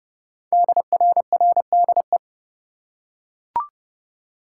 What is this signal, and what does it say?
Morse code, heard as a keyed tone, about 28 wpm, 700 Hz. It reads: DRRDE
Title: Code Groups - 5 Letters - 30wpm